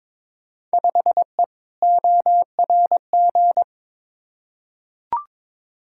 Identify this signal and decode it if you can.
Morse code, heard as a keyed tone, about 22 wpm, 700 Hz. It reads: 5E ORG
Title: Field Day Exchanges Encore - 22wpm